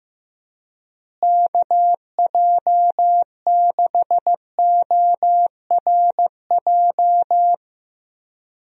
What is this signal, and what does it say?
Morse code, heard as a keyed tone, about 16 wpm, 700 Hz. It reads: KJ6ORJ
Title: USA Call Signs - 6 Characters - Repeated 2x - 15wpm